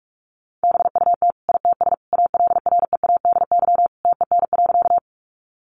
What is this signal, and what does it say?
Morse code, heard as a keyed tone, about 44 wpm, 700 Hz. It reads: BUT ITS ALREADY TEN.
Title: Sentences from Top 700 Words 45wpm